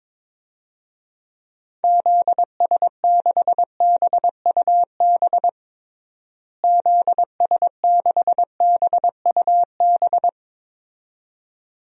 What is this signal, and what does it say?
Morse code, heard as a keyed tone, about 22 wpm, 700 Hz. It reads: ZS6BUB ZS6BUB
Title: DXCC-100 Call Signs - 6 Characters - Repeated 2x - 22wpm